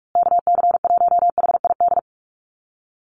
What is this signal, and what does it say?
Morse code, heard as a keyed tone, about 44 wpm, 690 Hz. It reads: KC1HID
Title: USA Call Signs Repeated 3x 45wpm